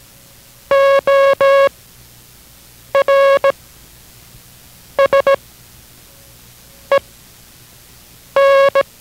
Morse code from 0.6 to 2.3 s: O